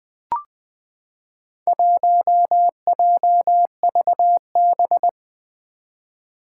Morse code, heard as a keyed tone, about 20 wpm, 700 Hz. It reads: 1JVB
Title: Four Challenging Letter-Number Combinations 20wpm